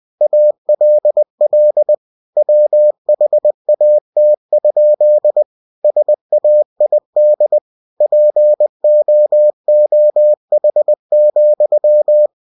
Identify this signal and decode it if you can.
Morse code, heard as a keyed tone, about 20 wpm, 600 Hz. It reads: ALL WHAT? SAID POOH,
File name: cw-books / winnie-the-pooh / winnie-20wpm-page-0045